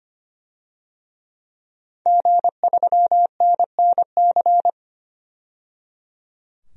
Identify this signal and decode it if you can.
Morse code, heard as a keyed tone, about 25 wpm, 700 Hz. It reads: G3NNC
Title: DXCC-100 Call Signs - 5 Characters - 25wpm